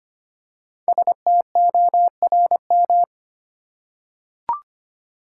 Morse code, heard as a keyed tone, about 25 wpm, 700 Hz. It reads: STORM
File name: 5 Letter Words - First 3 Letters Given - 25wpm